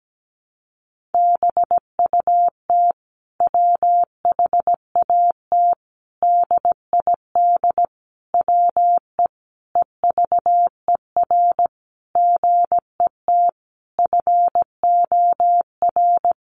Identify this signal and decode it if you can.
Morse code, heard as a keyed tone, about 17 wpm, 700 Hz. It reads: BUT WHAT DID WE EVER GET FOR
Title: Sentences from Top 400 Words 17wpm